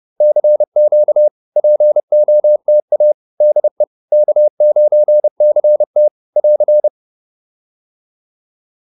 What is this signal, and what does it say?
Morse code, heard as a keyed tone, about 30 wpm, 600 Hz. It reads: CQ POTA DE K9CT <AR>